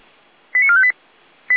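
Five-tone selective call (ZVEI), heard as quick runs of five tones, tones of about 70 ms. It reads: 89458
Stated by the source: ZVEI five-tone selective call received with an SDR.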